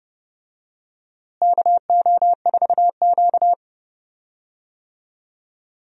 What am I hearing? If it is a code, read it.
Morse code, heard as a keyed tone, about 29 wpm, 700 Hz. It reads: KO4Q
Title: USA Call Signs - 4 Characters - Repeated 3x - 30wpm